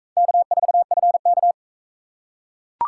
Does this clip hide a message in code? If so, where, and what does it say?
Yes: Morse code from 0.2 to 1.5 s, heard as a keyed tone, about 37 wpm, 700 Hz. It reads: K4FK